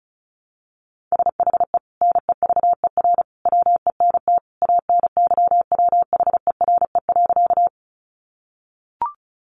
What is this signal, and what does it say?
Morse code, heard as a keyed tone, about 34 wpm, 700 Hz. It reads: SHE NEVER WENT ANYWHERE.
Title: Sentences from Top 1400 Words 35wpm